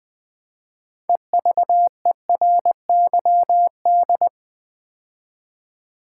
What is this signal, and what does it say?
Morse code, heard as a keyed tone, about 20 wpm, 700 Hz. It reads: EVERYD
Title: Build A Word - 2 Letters At A Time - 8-Letter Words - 20wpm